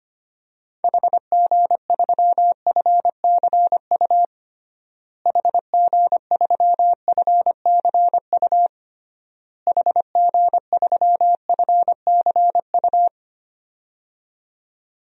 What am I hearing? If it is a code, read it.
Morse code, heard as a keyed tone, about 25 wpm, 700 Hz. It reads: HG3FCU HG3FCU HG3FCU
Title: DXCC-100 Call Signs - 6 Characters - Repeated 3x - 25wpm